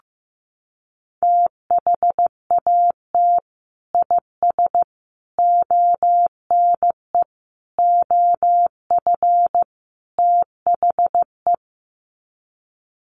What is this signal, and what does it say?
Morse code, heard as a keyed tone, about 15 wpm, 700 Hz. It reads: THAT IS ONE OF THE
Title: Sets of 5 Words Encore 15wpm